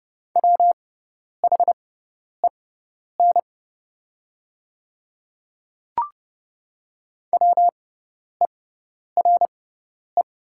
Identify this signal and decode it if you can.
Morse code, heard as a keyed tone, about 30 wpm, spaced as for 10 wpm, 700 Hz. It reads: WHEN WERE
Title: 6x Character Spacing - Top 100 Words - 30wpm